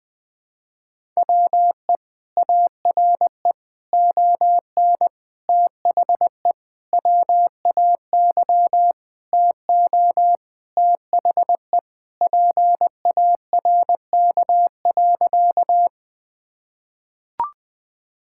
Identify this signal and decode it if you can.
Morse code, heard as a keyed tone, about 20 wpm, 700 Hz. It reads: WE ARE ON THE WAY TO THE PARK.
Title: Sentences with 4-Word Phrases - Spoken Beginning - 20wpm